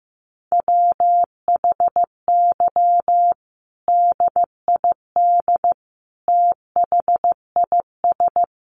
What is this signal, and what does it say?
Morse code, heard as a keyed tone, about 15 wpm, 700 Hz. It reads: WHY DID THIS
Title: Sentences from Top 900 Words 15wpm